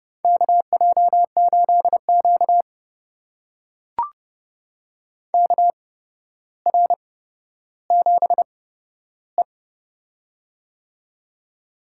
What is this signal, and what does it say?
Morse code, heard as a keyed tone, about 30 wpm, spaced as for 13 wpm, 700 Hz. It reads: KJ8Q KR7E
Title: USA Call Signs - 4 Characters - 8x Character Spacing - 30wpm